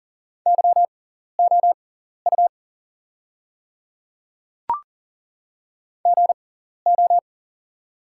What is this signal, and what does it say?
Morse code, heard as a keyed tone, about 40 wpm, spaced as for 14 wpm, 700 Hz. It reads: YOU GO
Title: 6x Character Spacing - Top 500 Words - 40wpm